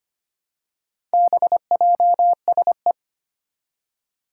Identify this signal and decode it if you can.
Morse code, heard as a keyed tone, about 25 wpm, 700 Hz. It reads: BJSE
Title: Four Challenging Letter-Number Combinations 25wpm